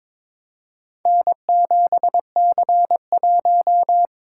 Morse code, heard as a keyed tone, about 23 wpm, 700 Hz. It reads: N7C1
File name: practice / 22wpm / Four Letter-Number Combinations - 22wpm